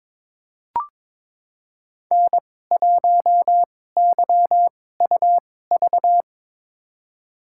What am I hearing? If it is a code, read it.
Morse code, heard as a keyed tone, about 22 wpm, spaced as for 16 wpm, 700 Hz. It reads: N1YUV
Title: USA Call Signs - 5 Characters - 2x Character Spacing - 22wpm